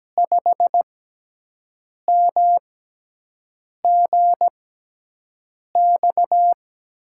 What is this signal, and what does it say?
Morse code, heard as a keyed tone, about 17 wpm, 700 Hz. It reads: 5 M G X
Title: USA Call Signs - 6 Characters - 6x Character Spacing - 17wpm